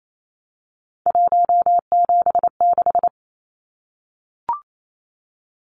Morse code, heard as a keyed tone, about 28 wpm, 700 Hz. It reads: 176
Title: Numbers - 3-Digit 28wpm